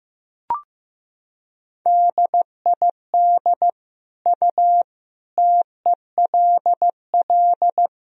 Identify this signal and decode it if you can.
Morse code, heard as a keyed tone, about 16 wpm, 700 Hz. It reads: DID U TELL
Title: Sentences from Top 200 Words and Abbreviations 15wpm